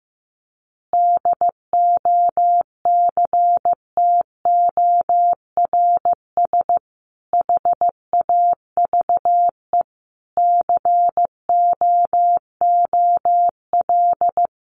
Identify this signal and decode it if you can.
Morse code, heard as a keyed tone, about 15 wpm, 700 Hz. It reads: DOCTORS HAVE COOL
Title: Easy Phrases 15wpm